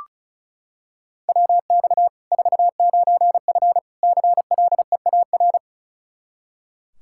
Morse code, heard as a keyed tone, about 34 wpm, 700 Hz. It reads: WX 49F CLEAR
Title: Practice QSO - Exchange 2 - Info - 35wpm